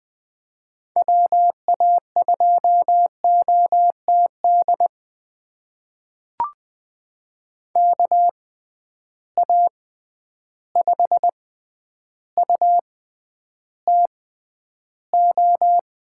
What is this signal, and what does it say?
Morse code, heard as a keyed tone, about 20 wpm, spaced as for 10 wpm, 700 Hz. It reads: WA2OTD KA5UTO
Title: USA Call Signs - 6 Characters - 6x Character Spacing - 20wpm